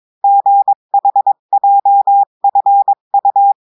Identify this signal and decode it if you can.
Morse code, heard as a keyed tone, about 22 wpm, 800 Hz. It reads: GHJFU
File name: cours-du-mercredi / audios / cours-853-du-2-mai-2024-etxte chiffre et texte clair-a-20-wpm